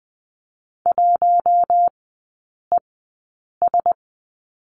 Morse code, heard as a keyed tone, about 20 wpm, spaced as for 8 wpm, 700 Hz. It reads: 1ES
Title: 053 Lesson 10e - Letter r - Three Characters 20wpm